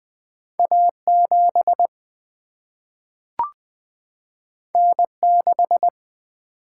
Morse code, heard as a keyed tone, about 20 wpm, 700 Hz. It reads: A7 N6
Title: Single-Letter Single-Number 20wpm